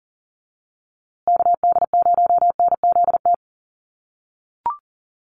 Morse code, heard as a keyed tone, about 40 wpm, 700 Hz. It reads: KD0NZT
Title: USA Call Signs Repeated 3x 40wpm